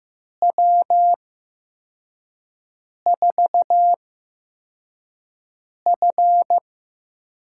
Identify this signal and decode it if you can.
Morse code, heard as a keyed tone, about 15 wpm, spaced as for 4 wpm, 700 Hz. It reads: W4F
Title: USA Call Signs - 8x Character Spacing - 15wpm